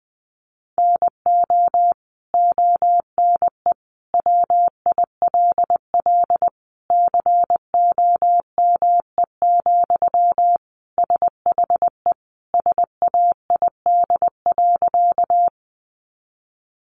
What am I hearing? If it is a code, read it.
Morse code, heard as a keyed tone, about 20 wpm, 700 Hz. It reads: NO ONE WILL COME, SHE SAID.
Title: Sentences from Top 100 Words 20wpm